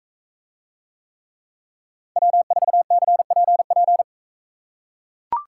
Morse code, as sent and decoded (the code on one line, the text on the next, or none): .-- ....- -.-. .--. .--.
W4CPP